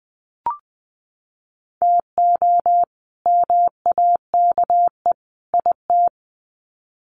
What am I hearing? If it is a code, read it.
Morse code, heard as a keyed tone, about 20 wpm, 700 Hz. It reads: TO MAKE IT
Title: Sets of 3 Words 20wpm